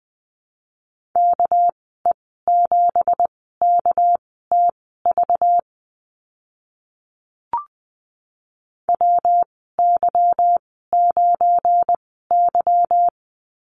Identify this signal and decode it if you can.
Morse code, heard as a keyed tone, about 20 wpm, spaced as for 14 wpm, 700 Hz. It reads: KE7KTV WY9Y
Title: 238 Lesson 38g - Letter q - Call Signs 20wpm